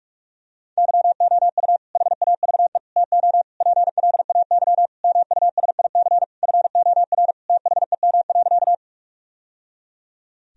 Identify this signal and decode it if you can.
Morse code, heard as a keyed tone, about 47 wpm, 700 Hz. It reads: YOU HAVE TO PLAY MUSIC FOR THEM.
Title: Sentences from Top 1000 Words - Review Part 1 - 45wpm